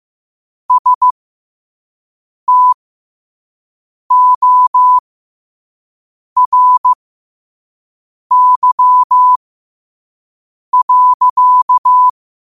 Morse code, heard as a keyed tone, about 15 wpm, spaced as for 5 wpm, 1000 Hz. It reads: STORY.
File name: qotdfiles / 5wpm